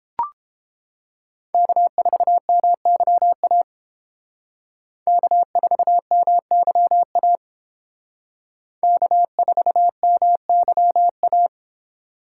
Morse code, heard as a keyed tone, about 31 wpm, 700 Hz. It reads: K4MYA K4MYA K4MYA